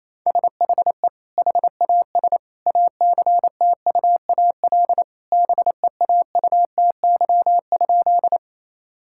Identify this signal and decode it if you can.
Morse code, heard as a keyed tone, about 28 wpm, 700 Hz. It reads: SHE HAS ACTUAL BEAUTY?
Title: Sentences from Top 1000 Words - Review Part 5 - 28wpm